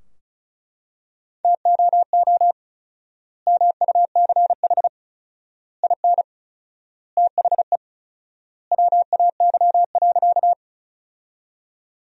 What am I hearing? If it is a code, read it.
Morse code, heard as a keyed tone, about 36 wpm, 700 Hz. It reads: TOO MUCH IN THE WAY.
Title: 4x Word-Spacing - Sentences from Top 200 Words - 35wpm